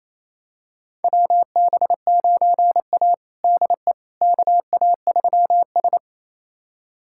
Morse code, heard as a keyed tone, about 27 wpm, 700 Hz. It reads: WB9A DE KA3S
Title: USA Callsign DE Callsign - 4 Characters - 28wpm